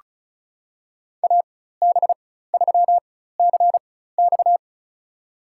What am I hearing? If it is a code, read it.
Morse code, heard as a keyed tone, about 36 wpm, spaced as for 16 wpm, 700 Hz. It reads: AB3CX
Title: USA Call Signs - 4x Character Spacing - 35wpm